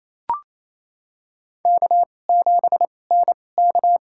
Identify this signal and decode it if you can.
Morse code, heard as a keyed tone, about 28 wpm, spaced as for 20 wpm, 700 Hz. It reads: K7NK
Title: USA Call Signs - 4 Characters - 2x Character Spacing - 28wpm